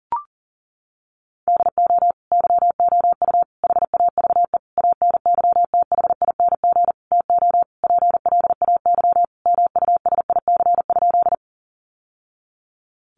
Morse code, heard as a keyed tone, about 40 wpm, 700 Hz. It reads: DO YOU HAVE ANYTHING TO PLAY MUSIC?
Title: Sentences from Top 1000 Words 40wpm